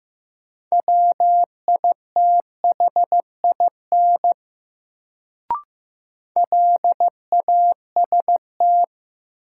Morse code, heard as a keyed tone, about 15 wpm, 700 Hz. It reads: WITHIN LAST
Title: Sets of 2 Words - New Word Drill - 15wpm